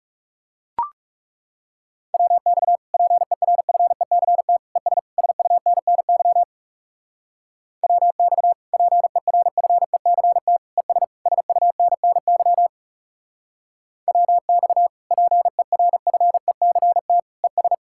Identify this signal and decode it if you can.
Morse code, heard as a keyed tone, about 38 wpm, 700 Hz. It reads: WX PERFECT ES SUNNY WX PERFECT ES SUNNY WX PERFECT ES